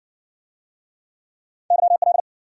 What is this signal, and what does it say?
Morse code, heard as a keyed tone, about 59 wpm, 700 Hz. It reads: YL